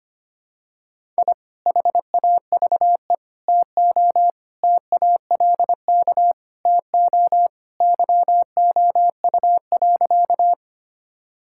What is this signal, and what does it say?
Morse code, heard as a keyed tone, about 25 wpm, 700 Hz. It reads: I HAVE TO TALK TO YOU.
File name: practice / 25wpm / Sentences with 3-Word Phrases - 25wpm